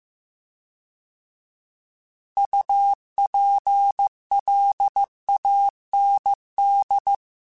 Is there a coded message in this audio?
Yes: Morse code from 2.4 to 7.2 s, heard as a keyed tone, about 15 wpm, 800 Hz. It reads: UPLAND